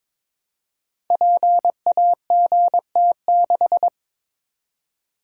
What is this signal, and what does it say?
Morse code, heard as a keyed tone, about 22 wpm, 700 Hz. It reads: PAGT6
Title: Code Groups - 5 Letter:Numbers - 22wpm